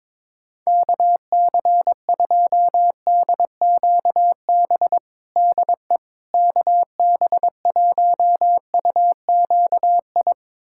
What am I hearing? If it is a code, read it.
Morse code, heard as a keyed tone, about 22 wpm, 700 Hz. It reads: KC2DQB DE KB1UQI
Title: USA Callsign DE Callsign 22wpm